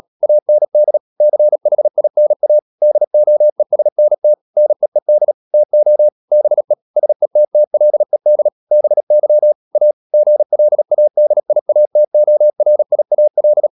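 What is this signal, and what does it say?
Morse code, heard as a keyed tone, about 37 wpm, 600 Hz. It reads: AND CHINA DOESNT NEED TO BE SETTLED BY A GLADIATORIAL
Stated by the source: Morse code practice at 35–39 wpm sending English text of Ken Follett's novel Never, from CW med Gnister.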